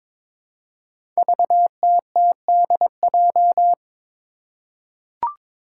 Morse code, heard as a keyed tone, about 22 wpm, 700 Hz. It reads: VTTDJ
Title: Five Challenging Letter-Number Combinations 22wpm